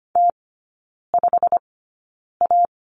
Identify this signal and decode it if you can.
Morse code, heard as a keyed tone, about 25 wpm, spaced as for 9 wpm, 690 Hz. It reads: T5A
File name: Lesson 15e - Number 5 - Three Characters 25wpm